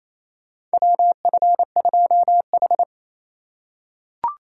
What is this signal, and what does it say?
Morse code, heard as a keyed tone, about 28 wpm, 700 Hz. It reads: WF2H
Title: USA Call Signs - 4 Characters - 2x Character Spacing - 28wpm